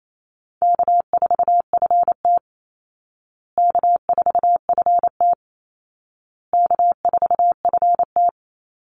Morse code, heard as a keyed tone, about 28 wpm, 700 Hz. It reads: K4FT K4FT K4FT
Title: Easy Call Signs Repeated 3x - 28wpm